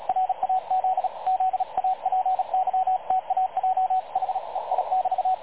Morse code, heard as a keyed tone, about 38 wpm, 750 Hz, with radio noise. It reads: RAZ ZAPYTAJ HEI <BT>
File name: dlazaawansowanych / nagraniazpasma / websdr recording 3567.0kHz